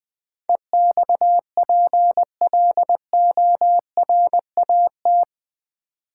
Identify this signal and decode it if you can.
Morse code, heard as a keyed tone, about 20 wpm, 700 Hz. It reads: EXPLORAT
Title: Build A Word - 4 Letters At A Time - 11-Letter Words - 20wpm